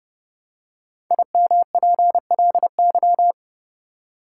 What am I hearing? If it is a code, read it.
Morse code, heard as a keyed tone, about 29 wpm, 700 Hz. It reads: IMPLY
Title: Five Letter Words - 30wpm